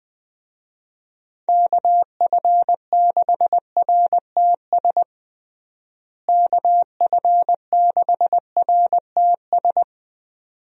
Morse code, heard as a keyed tone, about 20 wpm, 700 Hz. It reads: KF6RTS KF6RTS
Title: USA Call Signs - 6 Characters - Repeated 2x - 20wpm